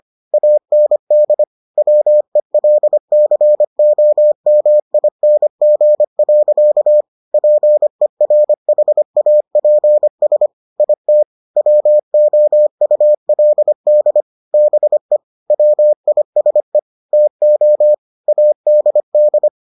Morse code, heard as a keyed tone, about 25 wpm, 600 Hz. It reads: AND WELCOMING. PERHAPS IT WOULD BE WISE TO ADD